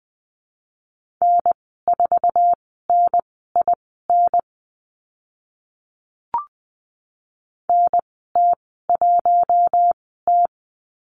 Morse code, heard as a keyed tone, about 20 wpm, spaced as for 15 wpm, 700 Hz. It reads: N4NIN NT1T
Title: 048 Lesson 9g - Number 4 - Call Signs 20wpm